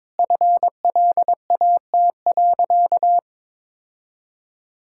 Morse code, heard as a keyed tone, about 22 wpm, 700 Hz. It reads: FLAT.
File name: practice / 22wpm / Easy Sentences with Hints - 22wpm